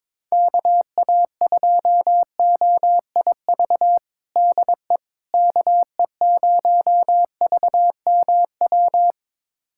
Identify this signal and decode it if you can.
Morse code, heard as a keyed tone, about 22 wpm, 700 Hz. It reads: KA2OIV DE KE0VMW
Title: USA Callsign DE Callsign - 6 Characters - 22wpm